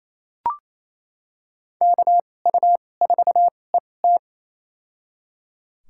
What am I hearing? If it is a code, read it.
Morse code, heard as a keyed tone, about 28 wpm, spaced as for 20 wpm, 700 Hz. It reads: KU4ET
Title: USA Call Signs - 2x Character Spacing - 28wpm